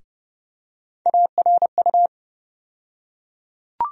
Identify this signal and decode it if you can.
Morse code, heard as a keyed tone, about 30 wpm, 700 Hz. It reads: ARU
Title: USA Call Sign Postfixes - 3 Letters 30wpm